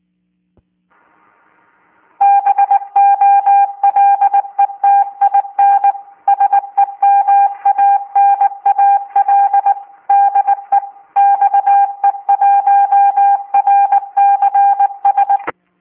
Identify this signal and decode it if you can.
Morse code, heard as a keyed tone, about 19 wpm, 800 Hz. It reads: BOLETIN SEMANAL DE XE1RCS